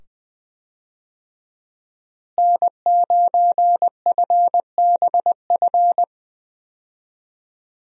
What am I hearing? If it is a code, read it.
Morse code, heard as a keyed tone, about 21 wpm, 700 Hz. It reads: N9FBF
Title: USA Call Signs - 5 Characters - Repeated 2x - 20wpm